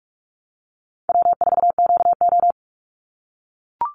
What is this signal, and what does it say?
Morse code, heard as a keyed tone, about 44 wpm, 700 Hz. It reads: W4QO
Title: Call Signs - 45wpm